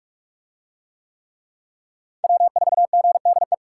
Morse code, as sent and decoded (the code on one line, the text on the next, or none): .-- ....- --. -.. .
W4GDE